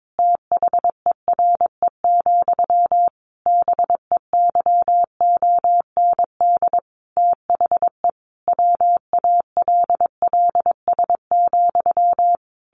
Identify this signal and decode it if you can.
Morse code, heard as a keyed tone, about 22 wpm, 700 Hz. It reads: THERE, BEYOND THE WALLS,
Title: A Sound Of Thunder 22wpm